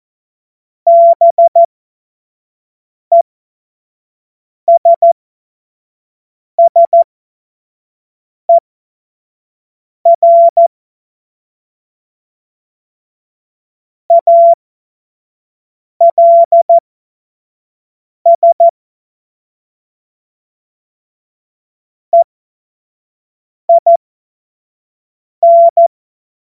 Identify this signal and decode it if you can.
Morse code, heard as a keyed tone, about 14 wpm, spaced as for 5 wpm, 680 Hz. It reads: BESSER ALS EIN
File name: Morsekurs / Witze / Witz 14. Dezember 2021, 25 F70